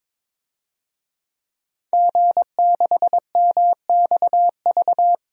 Morse code, heard as a keyed tone, about 22 wpm, 700 Hz. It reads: G6MXV
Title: DXCC-100 Call Signs - 5 Characters - Repeated 3x - 22wpm